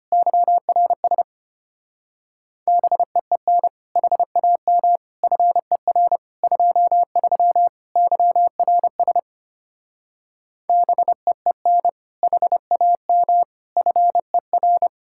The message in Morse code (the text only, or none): YRS BEEN HAM FER 23 YRS BEEN HAM FER